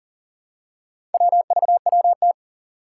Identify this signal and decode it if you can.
Morse code, heard as a keyed tone, about 42 wpm, 700 Hz. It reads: WVWT
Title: Four Challenging Letter-Number Combinations 40wpm